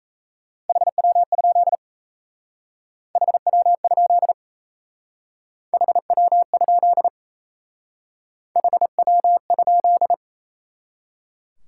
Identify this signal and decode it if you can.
Morse code, heard as a keyed tone, about 35 wpm, 700 Hz. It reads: HW? HW? HW? HW?
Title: Speed-Racing Top 90 QSO Elements 28wpm